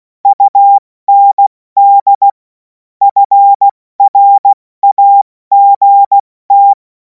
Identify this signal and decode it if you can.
Morse code, heard as a keyed tone, about 16 wpm, spaced as for 14 wpm, 800 Hz. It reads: UND FRAGT
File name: Morsekurs / Witze / Witz 21. April 2026 70 F80